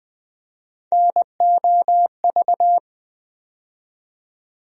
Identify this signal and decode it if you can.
Morse code, heard as a keyed tone, about 20 wpm, 700 Hz. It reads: NOV
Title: USA Call Sign Postfixes - 3 Letters 20wpm